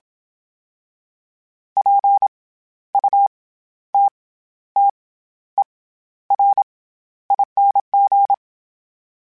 Morse code, heard as a keyed tone, about 27 wpm, 800 Hz. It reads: P U T T E R ING
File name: Special INGs-mp3s / Special INGs-mp3s / SlowToFaster ing @25